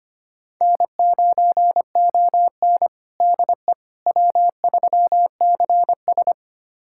Morse code, heard as a keyed tone, about 25 wpm, 700 Hz. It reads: N9ON DE W3CS
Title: USA Callsign DE Callsign - 4 Characters - 25wpm